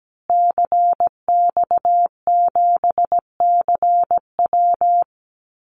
Morse code, heard as a keyed tone, about 17 wpm, 700 Hz. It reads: CX7CW
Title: Call Signs Repeated 3x - 17wpm